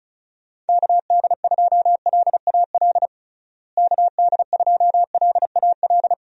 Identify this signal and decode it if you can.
Morse code, heard as a keyed tone, about 34 wpm, 700 Hz. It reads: KD2LAL KD2LAL
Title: USA Call Signs - 6 Characters - Repeated 2x - 35wpm